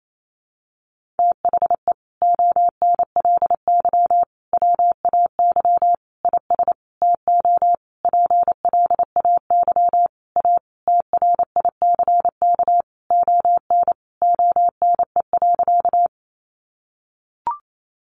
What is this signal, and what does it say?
Morse code, heard as a keyed tone, about 28 wpm, 700 Hz. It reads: THE ONLY WAY IS TO PLAY A TRICK ON ONE.
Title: Sentences from Top 1500 Words 28wpm